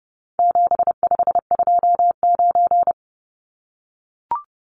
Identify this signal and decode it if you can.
Morse code, heard as a keyed tone, about 28 wpm, 700 Hz. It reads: 7529
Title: Numbers - 4-Digit 30wpm